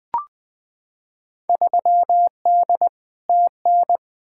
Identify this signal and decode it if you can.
Morse code, heard as a keyed tone, about 20 wpm, 700 Hz. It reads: 3D TN
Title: Field Day Exchanges Encore - 20wpm